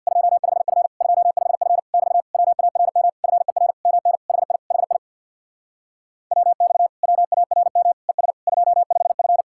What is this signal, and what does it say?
Morse code, heard as a keyed tone, about 67 wpm, 700 Hz. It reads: 25F 25F <BT> WARM FER MN HI HI WX WARM ES 25F